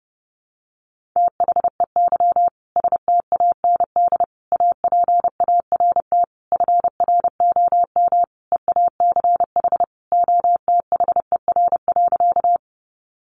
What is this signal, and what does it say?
Morse code, heard as a keyed tone, about 30 wpm, 700 Hz. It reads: THEY STAND APART FROM EACH OTHER.
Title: Easy Phrases 30wpm